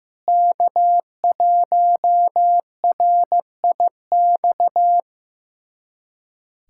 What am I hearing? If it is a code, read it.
Morse code, heard as a keyed tone, about 16 wpm, 700 Hz. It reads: K1RIX
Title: USA Callsign DE Callsign 15wpm